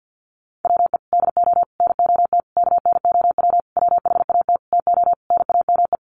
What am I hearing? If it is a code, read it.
Morse code, heard as a keyed tone, about 50 wpm, 700 Hz. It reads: WE DO NOT KNOW WHAT TO NAME
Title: Sentences from Top 1500 Words 50wpm